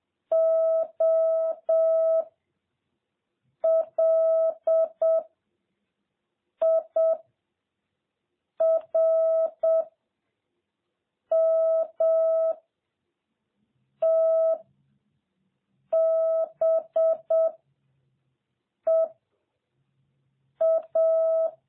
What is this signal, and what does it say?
Morse code, heard as a keyed tone, about 7 wpm, 650 Hz. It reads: O L I R M T B E A
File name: jh8cbh / 2021cw